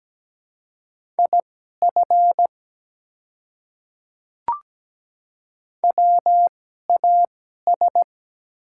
Morse code, heard as a keyed tone, about 17 wpm, spaced as for 12 wpm, 700 Hz. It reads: IF WAS
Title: 2x Character Spacing - Top 200 Words - 17wpm